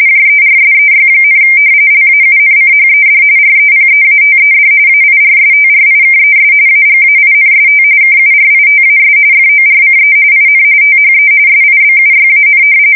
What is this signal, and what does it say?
RTTY, about 45.45 baud. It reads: JERSEY, LAUNCHED REPEATED AIRSTRIKES AGAINST KURITAS FORCE IN THE SIBUYAN
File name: fsk170 / battleshipnj